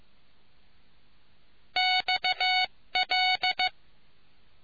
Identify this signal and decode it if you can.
Morse code, heard as a keyed tone, about 15 wpm, 740 Hz. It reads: XL